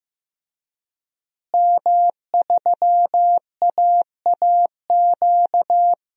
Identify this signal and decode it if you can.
Morse code, heard as a keyed tone, about 15 wpm, 700 Hz. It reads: M3AAQ
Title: DXCC-100 Call Signs - 5 Characters - Repeated 3x - 15wpm